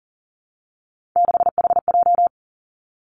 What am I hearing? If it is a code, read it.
Morse code, heard as a keyed tone, about 40 wpm, 700 Hz. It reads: 6HJ
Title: Three Challenging Letter-Number Combinations 40wpm